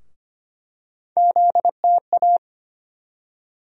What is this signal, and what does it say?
Morse code, heard as a keyed tone, about 25 wpm, 700 Hz. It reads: ZTA
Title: USA Call Sign Postfixes - 3 Letters 25wpm